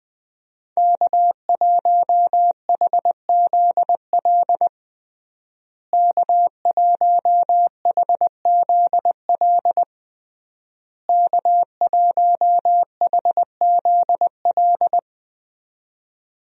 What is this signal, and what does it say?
Morse code, heard as a keyed tone, about 20 wpm, 700 Hz. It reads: K1HZL K1HZL K1HZL
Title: USA Call Signs - 5 Characters - Repeated 3x - 20wpm